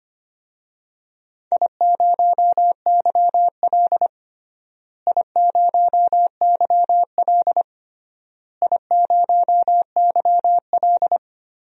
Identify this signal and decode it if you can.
Morse code, heard as a keyed tone, about 25 wpm, 700 Hz. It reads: I0YL I0YL I0YL
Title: DXCC-100 Call Signs - 4 Characters - Repeated 3x - 25wpm